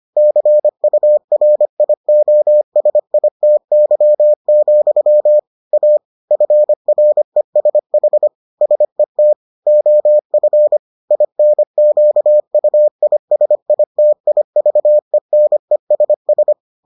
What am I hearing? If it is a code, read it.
Morse code, heard as a keyed tone, about 25 wpm, 600 Hz. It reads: CURIOSITY, A FRESH SET OF INQUISITIVENESS